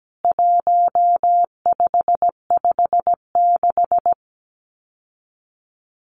Morse code, heard as a keyed tone, about 17 wpm, 700 Hz. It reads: 1556
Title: CWT Exchange 17wpm